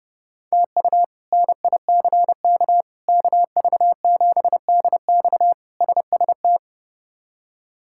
Morse code, heard as a keyed tone, about 31 wpm, 700 Hz. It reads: TU NICK KV7DX SST
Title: K1USN SST QSO Practice - 30wpm